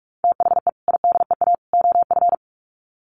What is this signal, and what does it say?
Morse code, heard as a keyed tone, about 44 wpm, 700 Hz. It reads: THE IDEA OF
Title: Sets of 3 Words 45wpm